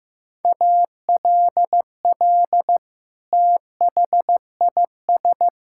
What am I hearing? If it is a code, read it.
Morse code, heard as a keyed tone, about 15 wpm, 700 Hz. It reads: ALL THIS
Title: Sentences from Top 1000 Words - Review Part 6 - 15wpm